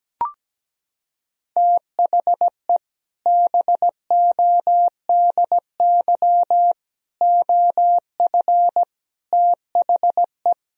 Morse code, heard as a keyed tone, about 17 wpm, 700 Hz. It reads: THE BODY OF THE
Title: Sentences from Top 1000 Words - Review Part 2 - 17wpm